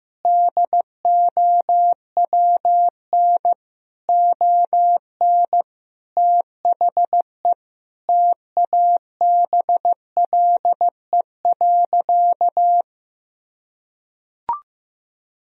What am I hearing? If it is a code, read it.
Morse code, heard as a keyed tone, about 15 wpm, 700 Hz. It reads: DOWN ON THE TABLE.
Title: Sentences from Top 1000 Words - Review Part 3 - 15wpm